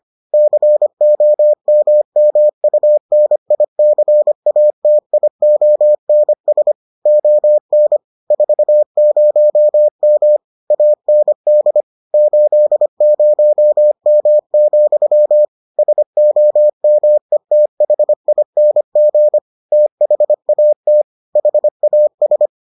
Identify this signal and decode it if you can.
Morse code, heard as a keyed tone, about 25 wpm, 600 Hz. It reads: COMMUNICATIONS ON 40M AND 80M, SOMETHING THAT HAS